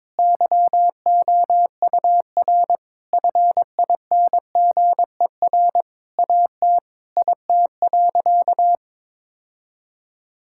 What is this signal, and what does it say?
Morse code, heard as a keyed tone, about 22 wpm, 700 Hz. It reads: YOUR FINGER AT IT.
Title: Sentences from Top 1600 Words 22wpm